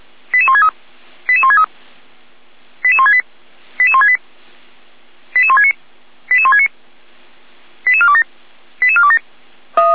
Five-tone selective call (ZVEI), heard as quick runs of five tones, tones of about 70 ms. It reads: 88163 88163 88168 88168 88169 88169 88427 88427
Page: ZVEI